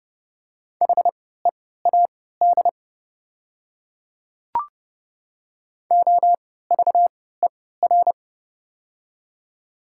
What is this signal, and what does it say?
Morse code, heard as a keyed tone, about 29 wpm, spaced as for 17 wpm, 700 Hz. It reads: HEAD OVER
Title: 3x Character Spacing - Top 500 Words - 30wpm